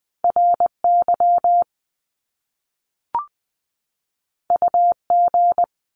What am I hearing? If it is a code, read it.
Morse code, heard as a keyed tone, about 20 wpm, 700 Hz. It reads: RY UG
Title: Top Two Letter Combinations in Words 20wpm